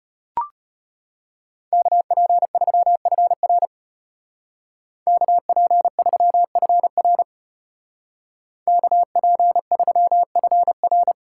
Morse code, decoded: KP3FR KP3FR KP3FR

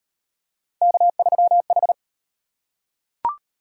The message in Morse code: K3H

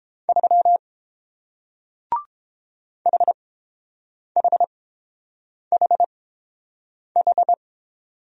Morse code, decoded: 3 HHHH

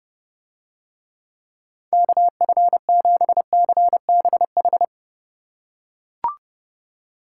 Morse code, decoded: KF7CBH

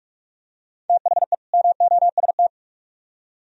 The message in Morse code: THE MOST

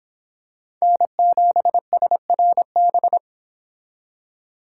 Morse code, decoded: N7SRB

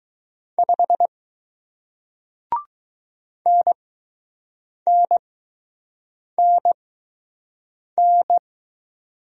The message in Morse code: 5 NNNN

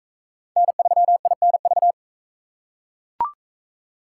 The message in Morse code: N3INV

